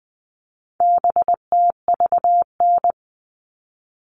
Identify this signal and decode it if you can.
Morse code, heard as a keyed tone, about 20 wpm, 700 Hz. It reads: BTVN
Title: All Cut Numbers - 4-Digit 20wpm